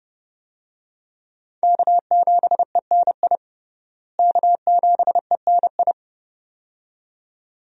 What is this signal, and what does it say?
Morse code, heard as a keyed tone, about 31 wpm, 700 Hz. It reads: K7ENI K7ENI
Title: USA Call Signs - 5 Characters - Repeated 2x - 30wpm